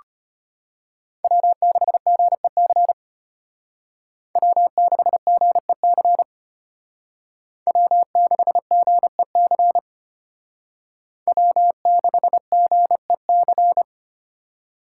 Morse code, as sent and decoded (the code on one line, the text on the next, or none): .-- -.... --. -.-. / .-- -.... --. -.-. / .-- -.... --. -.-. / .-- -.... --. -.-.
W6GEC W6GEC W6GEC W6GEC